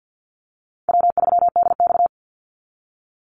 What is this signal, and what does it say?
Morse code, heard as a keyed tone, about 48 wpm, 700 Hz. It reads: W3DX
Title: NA SOTA Activator Callsigns 50wpm